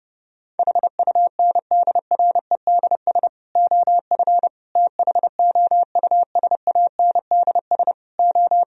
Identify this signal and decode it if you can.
Morse code, heard as a keyed tone, about 31 wpm, 700 Hz. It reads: HUNDREDS OF THOUSANDS O